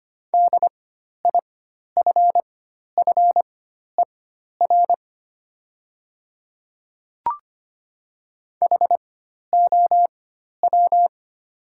Morse code, heard as a keyed tone, about 25 wpm, spaced as for 12 wpm, 690 Hz. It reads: DIFFER HOW
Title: 4x Character Spacing - Top 200 Words - 25wpm